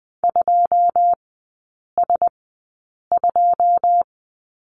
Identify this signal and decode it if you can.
Morse code, heard as a keyed tone, about 20 wpm, spaced as for 8 wpm, 700 Hz. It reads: 2S2
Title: 121 Lesson 20e - Number 3 - Three Characters 20wpm